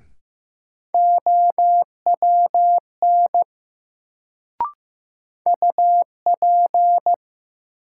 Morse code, heard as a keyed tone, about 15 wpm, 700 Hz. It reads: OWN UP
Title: Sets of 2 Words - New Word Drill - 15wpm